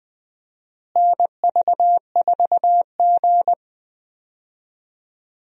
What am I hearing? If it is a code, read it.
Morse code, heard as a keyed tone, about 20 wpm, 700 Hz. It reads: NV4G
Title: USA Call Signs - 4 Characters - Repeated 3x - 20wpm